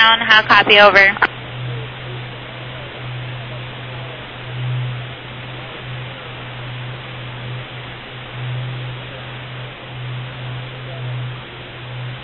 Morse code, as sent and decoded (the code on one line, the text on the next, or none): none